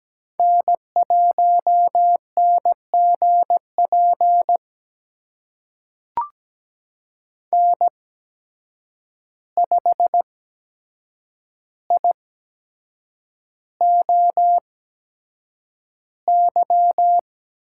Morse code, heard as a keyed tone, about 17 wpm, spaced as for 7 wpm, 700 Hz. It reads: N1NGP N5IOY
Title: USA Call Signs - 5 Characters - 8x Character Spacing - 17wpm